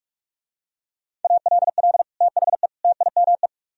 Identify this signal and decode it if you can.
Morse code, heard as a keyed tone, about 44 wpm, 700 Hz. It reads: ALL THE TIME